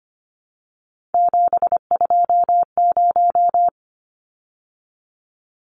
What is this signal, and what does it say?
Morse code, heard as a keyed tone, about 25 wpm, 700 Hz. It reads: 720
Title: Numbers - 3-Digit 25wpm